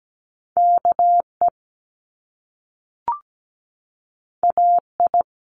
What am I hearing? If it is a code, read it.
Morse code, heard as a keyed tone, about 17 wpm, 700 Hz. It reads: KE AI
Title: Top Two Letter Combinations in Words 17wpm